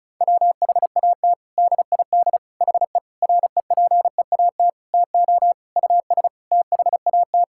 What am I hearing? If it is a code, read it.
Morse code, heard as a keyed tone, about 34 wpm, 700 Hz. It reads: WHAT DID HE REPEAT TO US THAT
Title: Sentences from Top 1600 Words 35wpm